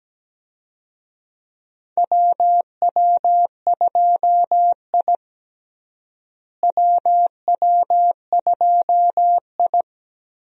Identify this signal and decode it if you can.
Morse code, heard as a keyed tone, about 17 wpm, 700 Hz. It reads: WW2I WW2I
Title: USA Call Signs - 4 Characters - Repeated 2x - 17wpm